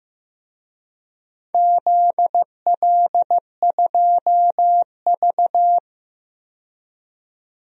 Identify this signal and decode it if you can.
Morse code, heard as a keyed tone, about 15 wpm, 700 Hz. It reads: ZL2V
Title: DXCC-100 Call Signs - 4 Characters - 15wpm